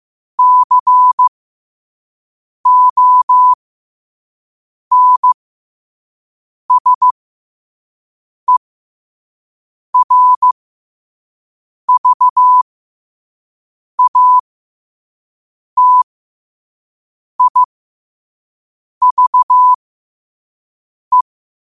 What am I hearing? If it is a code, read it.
Morse code, heard as a keyed tone, about 15 wpm, spaced as for 5 wpm, 1000 Hz. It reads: CONSERVATIVE
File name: qotdfiles / 5wpm